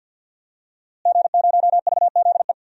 Morse code, heard as a keyed tone, about 51 wpm, 700 Hz. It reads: G0VZE
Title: DXCC-100 Call Signs - 5 Characters - Repeated 3x - 50wpm